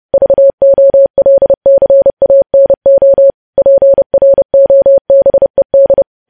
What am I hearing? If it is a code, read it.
Morse code, heard as a keyed tone, about 30 wpm, 550 Hz. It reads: VOLCANO PROBED
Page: News Headlines in Morse Code at 30 WPM